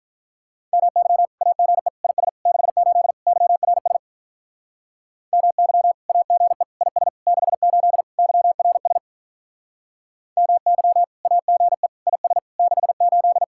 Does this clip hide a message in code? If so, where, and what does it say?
Yes: Morse code from 0.7 to 13.4 s, heard as a keyed tone, about 44 wpm, 700 Hz. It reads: MY AGE IS 68 YRS MY AGE IS 68 YRS MY AGE IS 68